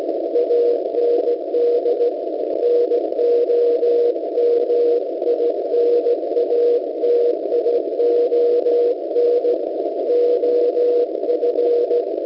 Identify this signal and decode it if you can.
Morse code, heard as a keyed tone, about 16 wpm, 480 Hz. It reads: AND NOMINATION OF